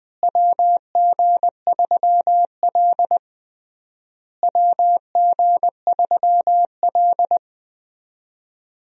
Morse code, heard as a keyed tone, about 20 wpm, 700 Hz. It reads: WG3L WG3L
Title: USA Call Signs - 4 Characters - Repeated 3x - 20wpm